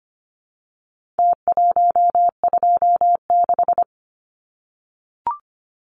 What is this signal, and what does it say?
Morse code, heard as a keyed tone, about 25 wpm, 700 Hz. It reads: T126
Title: Numbers - Contest Style - 4-Digit 25wpm